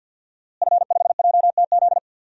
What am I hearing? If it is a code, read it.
Morse code, heard as a keyed tone, about 52 wpm, 700 Hz. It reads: F5JTZ
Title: DXCC-100 Call Signs - 5 Characters - Repeated 2x - 50wpm